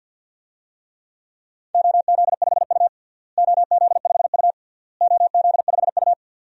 Morse code, heard as a keyed tone, about 53 wpm, 700 Hz. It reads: OZ5U OZ5U OZ5U
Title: DXCC-100 Call Signs - 4 Characters - Repeated 3x - 50wpm